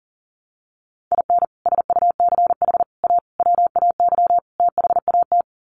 Morse code, heard as a keyed tone, about 40 wpm, 700 Hz. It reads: IN SUCH A WAY THAT
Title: Sets of 5 Words 40wpm